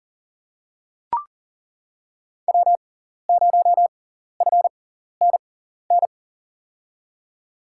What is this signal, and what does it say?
Morse code, heard as a keyed tone, about 40 wpm, spaced as for 14 wpm, 700 Hz. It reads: W0FNN
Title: USA Call Signs - 5 Characters - 6x Character Spacing - 40wpm